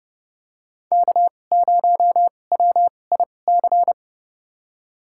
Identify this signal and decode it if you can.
Morse code, heard as a keyed tone, about 32 wpm, spaced as for 22 wpm, 700 Hz. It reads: K0WIC
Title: USA Call Signs - 5 Characters - 2x Character Spacing - 30wpm